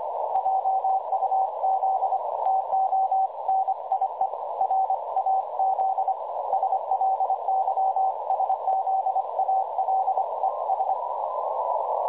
Morse code, heard as a keyed tone, about 25 wpm, 770 Hz, with noise. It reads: JUZ TO NIE RAZ SLYSZALES
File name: dlazaawansowanych / nagraniazpasma / websdr recording 3565.0kHz